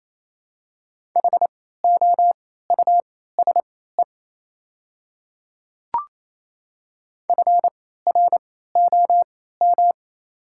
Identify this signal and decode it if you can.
Morse code, heard as a keyed tone, about 28 wpm, spaced as for 16 wpm, 700 Hz. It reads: HOUSE FROM
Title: 3x Character Spacing - Top 200 Words - 28wpm